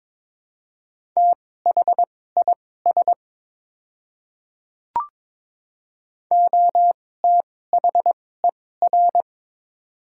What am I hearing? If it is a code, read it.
Morse code, heard as a keyed tone, about 22 wpm, spaced as for 16 wpm, 700 Hz. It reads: THIS OTHER
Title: 2x Character Spacing - Top 100 Words - 22wpm